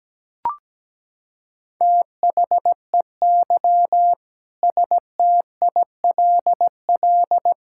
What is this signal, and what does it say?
Morse code, heard as a keyed tone, about 17 wpm, 700 Hz. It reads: THEY STILL
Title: Sentences from Top 1000 Words - Review Part 5 - 17wpm